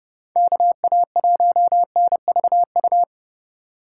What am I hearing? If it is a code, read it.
Morse code, heard as a keyed tone, about 29 wpm, 700 Hz. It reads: KA1NVU